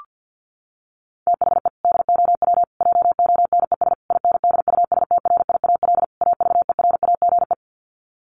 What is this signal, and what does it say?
Morse code, heard as a keyed tone, about 50 wpm, 700 Hz. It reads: THE DOW JONES INDUSTRIAL AVERAGE
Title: Sets of 5 Words 50wpm